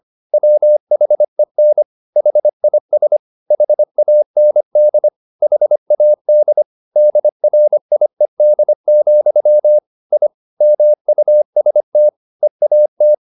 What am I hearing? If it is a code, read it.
Morse code, heard as a keyed tone, about 25 wpm, 600 Hz. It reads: WHEN HIS HAND HAD DRIED, I MUST EAT
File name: the old man and the sea part 1 25wpm